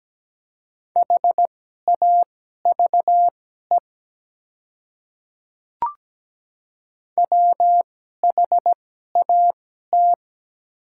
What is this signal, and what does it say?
Morse code, heard as a keyed tone, about 17 wpm, spaced as for 12 wpm, 700 Hz. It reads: HAVE WHAT
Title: 2x Character Spacing - Top 200 Words - 17wpm